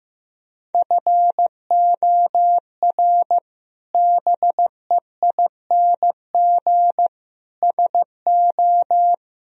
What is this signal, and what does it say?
Morse code, heard as a keyed tone, about 15 wpm, 700 Hz. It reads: FOR BEING SO
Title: Sentences from Top 1000 Words - Review Part 3 - 15wpm